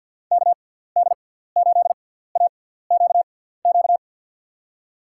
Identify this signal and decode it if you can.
Morse code, heard as a keyed tone, about 49 wpm, spaced as for 17 wpm, 700 Hz. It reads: KD8AQQ
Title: USA Call Signs - 6 Characters - 6x Character Spacing - 50wpm